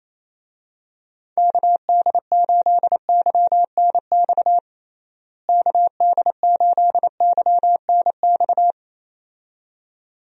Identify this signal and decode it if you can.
Morse code, heard as a keyed tone, about 27 wpm, 700 Hz. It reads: KD8YNX KD8YNX
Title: USA Call Signs - 6 Characters - Repeated 2x - 28wpm